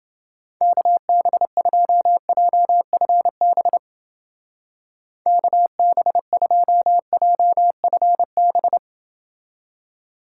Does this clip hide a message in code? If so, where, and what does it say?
Yes: Morse code from 0.6 to 8.8 s, heard as a keyed tone, about 31 wpm, 700 Hz. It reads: KB2JFB KB2JFB